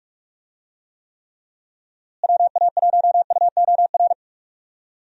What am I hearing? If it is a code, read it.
Morse code, heard as a keyed tone, about 48 wpm, 700 Hz. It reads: WA1UOR
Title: USA Call Signs - 6 Characters - Repeated 3x - 45wpm